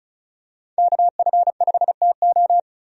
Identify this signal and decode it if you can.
Morse code, heard as a keyed tone, about 36 wpm, 700 Hz. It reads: KF5TO
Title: USA Call Signs - 5 Characters - 35wpm